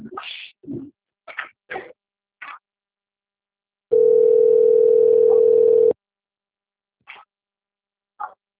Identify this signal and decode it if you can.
Ringback tone (North America), 440 + 480 Hz.